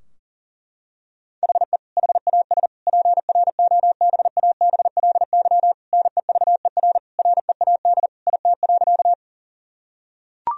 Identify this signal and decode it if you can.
Morse code, heard as a keyed tone, about 37 wpm, 700 Hz. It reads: HE HAS PROBABLY NEVER READ IT.
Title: Sentences from Top 1000 Words - Review Part 4 - 40wpm